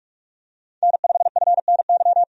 Morse code, heard as a keyed tone, about 42 wpm, 700 Hz. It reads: N5FNY
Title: USA Call Signs - 5 Characters - 45wpm